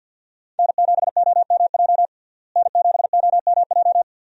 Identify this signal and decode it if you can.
Morse code, heard as a keyed tone, about 47 wpm, 700 Hz. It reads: N7OMJ N7OMJ
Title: USA Call Signs - 5 Characters - Repeated 2x - 50wpm